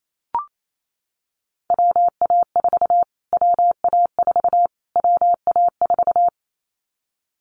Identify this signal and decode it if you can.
Morse code, heard as a keyed tone, about 28 wpm, 700 Hz. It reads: WA4 WA4 WA4
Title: Callsign Prefixes for Top 25 DXCC - Repeated 3x - 28wpm